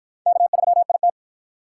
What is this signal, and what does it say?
Morse code, heard as a keyed tone, about 53 wpm, 700 Hz. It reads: K3IT